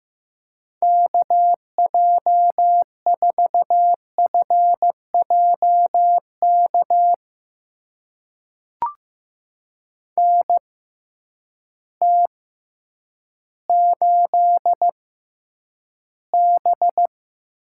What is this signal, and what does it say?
Morse code, heard as a keyed tone, about 15 wpm, spaced as for 9 wpm, 700 Hz. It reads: KJ4FJK NT8B
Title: USA Call Signs - 6x Character Spacing - 15wpm